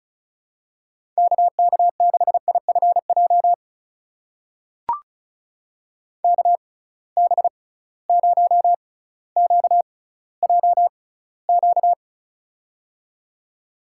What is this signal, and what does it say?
Morse code, heard as a keyed tone, about 35 wpm, spaced as for 18 wpm, 700 Hz. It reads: KK6IFJ KB0QJQ
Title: USA Call Signs - 6x Character Spacing - 35wpm